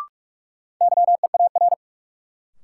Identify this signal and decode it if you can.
Morse code, heard as a keyed tone, about 46 wpm, 700 Hz. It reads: YEAR
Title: Sets of 2 Words - New Word Drill - 45wpm